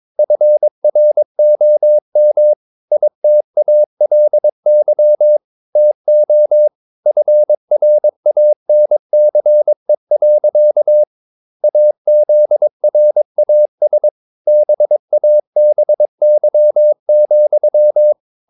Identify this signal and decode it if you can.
Morse code, heard as a keyed tone, about 22 wpm, 600 Hz. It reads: FROM ITALY TO FRANCE. AZRAS BABY,
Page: Romanen Never af Ken Follet på engelsk.